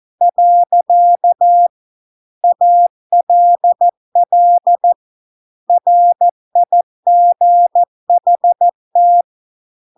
Morse code, heard as a keyed tone, about 14 wpm, 700 Hz. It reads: . ALL RIGHT